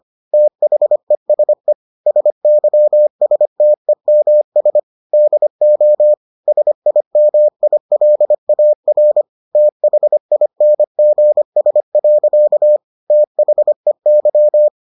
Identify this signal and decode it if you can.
Morse code, heard as a keyed tone, about 25 wpm, 600 Hz. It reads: THESE SYSTEMS DO SIMILAR THINGS. THEY